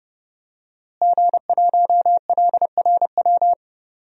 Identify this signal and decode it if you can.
Morse code, heard as a keyed tone, about 30 wpm, 700 Hz. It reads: G1LRW
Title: DXCC-100 Call Signs - 5 Characters - Repeated 3x - 30wpm